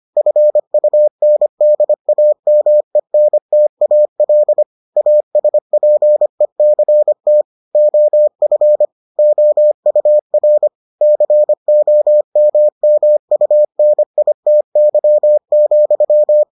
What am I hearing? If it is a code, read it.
Morse code, heard as a keyed tone, about 25 wpm, 600 Hz. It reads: FUNDAMENTAL ASPECT OF OUR COMMUNITY,